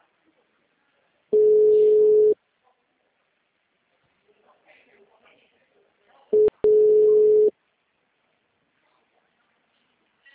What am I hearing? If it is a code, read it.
Ringback tone (Europe, 425 Hz), with speech.
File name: out- -2007- - - .36755